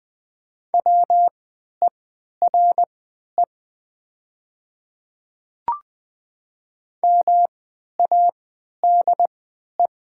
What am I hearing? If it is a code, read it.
Morse code, heard as a keyed tone, about 20 wpm, spaced as for 11 wpm, 690 Hz. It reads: WERE MADE
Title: 3x Character Spacing - Top 200 Words - 20wpm